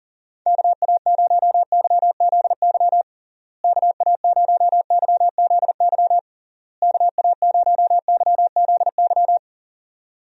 Morse code, heard as a keyed tone, about 37 wpm, 700 Hz. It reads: KA0YZY KA0YZY KA0YZY
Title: USA Call Signs - 6 Characters - Repeated 3x - 40wpm